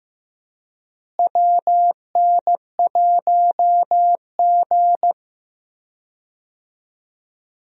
Morse code, heard as a keyed tone, about 15 wpm, 700 Hz. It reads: WN1G
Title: USA Call Signs - 4 Characters - Repeated 3x - 15wpm